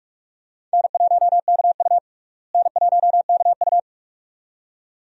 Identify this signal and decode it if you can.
Morse code, heard as a keyed tone, about 46 wpm, 700 Hz. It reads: N1KU N1KU
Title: USA Call Signs - 4 Characters - Repeated 2x - 45wpm